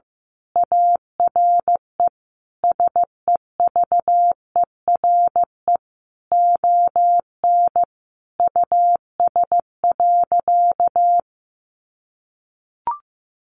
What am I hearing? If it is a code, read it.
Morse code, heard as a keyed tone, about 14 wpm, 700 Hz. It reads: ARE SEVERE ON US.
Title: Sentences from Top 1400 Words 15wpm